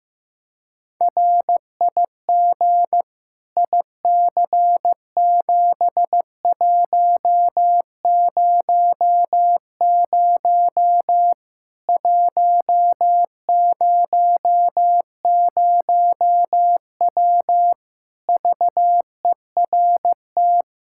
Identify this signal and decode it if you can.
Morse code, heard as a keyed tone, about 15 wpm, 700 Hz. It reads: RIG IC7100 100W VERT
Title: K1USN SST QSO Practice 15wpm